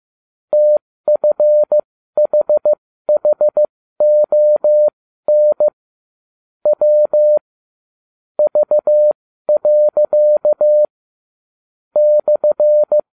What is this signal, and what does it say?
Morse code, heard as a keyed tone, about 15 wpm, spaced as for 13 wpm, 600 Hz. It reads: TFHHON W V. /
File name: cw / ubacw31 1